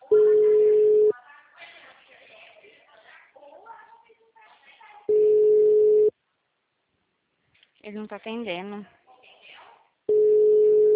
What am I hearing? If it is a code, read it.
Ringback tone (Europe, 425 Hz), with speech.